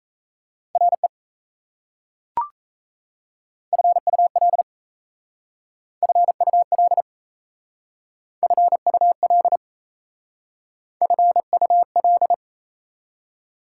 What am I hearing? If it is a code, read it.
Morse code, heard as a keyed tone, about 37 wpm, 700 Hz. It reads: RE FUL FUL FUL FUL